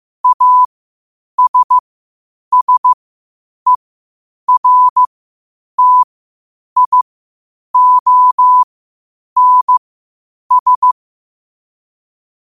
Morse code, heard as a keyed tone, about 15 wpm, spaced as for 8 wpm, 1000 Hz. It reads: ASSERTIONS
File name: qotdfiles / 8wpm